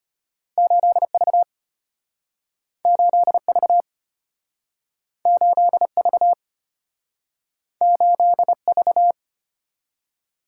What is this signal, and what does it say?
Morse code, heard as a keyed tone, about 34 wpm, 700 Hz. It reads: 8V 8V 8V 8V